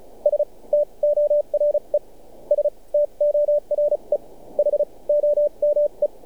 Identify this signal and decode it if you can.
Morse code, heard as a keyed tone, about 36 wpm, spaced as for 27 wpm, 590 Hz. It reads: STORE STORE HOME
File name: archived / CRSnet